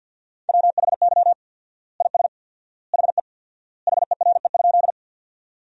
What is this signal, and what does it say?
Morse code, heard as a keyed tone, about 50 wpm, 700 Hz. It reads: WHY IS HE HERE?
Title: 4x Word-Spacing - Sentences from Top 200 Words - 50wpm